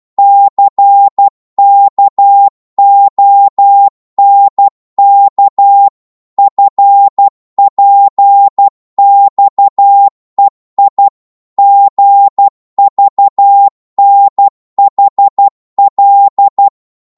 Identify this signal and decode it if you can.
Morse code, heard as a keyed tone, about 12 wpm, 800 Hz. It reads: CKONK FPXEI GVNHL